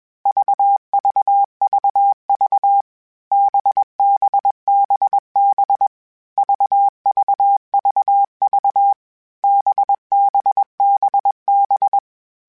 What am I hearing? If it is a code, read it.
Morse code, heard as a keyed tone, about 21 wpm, 800 Hz. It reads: VVVV BBBB VVVV BBBB